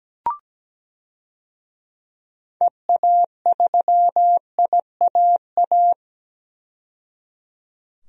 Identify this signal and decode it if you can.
Morse code, heard as a keyed tone, about 17 wpm, 700 Hz. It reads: EA3IAA
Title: DXCC-100 Call Signs - 6 Characters - 17wpm